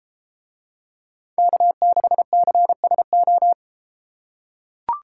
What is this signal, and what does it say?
Morse code, heard as a keyed tone, about 33 wpm, 700 Hz. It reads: K6CSO